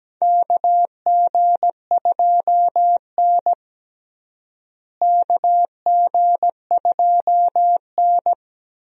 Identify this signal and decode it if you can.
Morse code, heard as a keyed tone, about 17 wpm, 700 Hz. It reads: KG2N KG2N
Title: USA Call Signs - 4 Characters - Repeated 3x - 17wpm